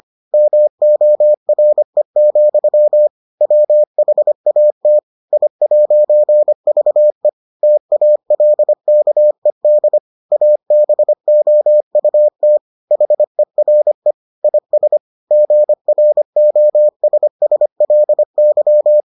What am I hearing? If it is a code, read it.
Morse code, heard as a keyed tone, about 25 wpm, 600 Hz. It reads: MORE, WHAT I'VE TALKED ABOUT HERE IS GROSSLY